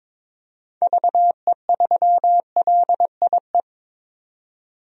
Morse code, heard as a keyed tone, about 22 wpm, 700 Hz. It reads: VE3LIE
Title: DXCC-100 Call Signs - 6 Characters - 22wpm